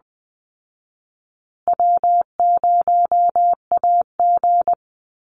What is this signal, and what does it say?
Morse code, heard as a keyed tone, about 20 wpm, 700 Hz. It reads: W0AG
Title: Call Signs Repeated 2x - 20wpm